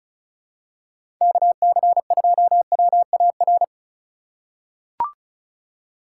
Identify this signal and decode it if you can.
Morse code, heard as a keyed tone, about 36 wpm, 700 Hz. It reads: KC2WAR
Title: USA Call Signs - 2x Character Spacing - 35wpm